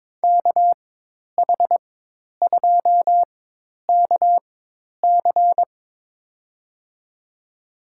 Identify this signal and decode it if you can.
Morse code, heard as a keyed tone, about 22 wpm, spaced as for 10 wpm, 700 Hz. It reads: KH2KC
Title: USA Call Signs - 5 Characters - 4x Character Spacing - 22wpm